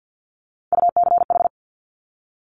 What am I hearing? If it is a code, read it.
Morse code, heard as a keyed tone, about 50 wpm, 700 Hz. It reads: UCH
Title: Top Three Letter Combinations in Words 50wpm